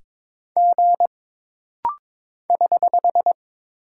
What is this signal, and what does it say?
Morse code, heard as a keyed tone, about 22 wpm, 700 Hz. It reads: G <HH>
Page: This rapid-fire timing is geared to push your ICR (Instant Character Recognition) skills.